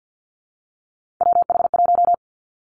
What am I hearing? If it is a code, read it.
Morse code, heard as a keyed tone, about 50 wpm, 700 Hz. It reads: WH1
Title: Three Challenging Letter-Number Combinations 50wpm